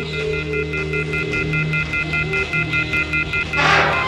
Off-hook warning tone, 1400 + 2060 + 2450 + 2600 Hz.
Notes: Mit einer Spielzeugpfeife, die er in seiner Müslipackung ( "Cap'n Crunch") fand, manipulierte er in den 70er-Jahren das weltweite Telefonnetz.